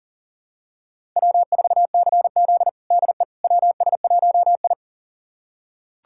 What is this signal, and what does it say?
Morse code, heard as a keyed tone, about 40 wpm, 700 Hz. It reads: W4CZ DE WS1I
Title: USA Callsign DE Callsign - 4 Characters - 40wpm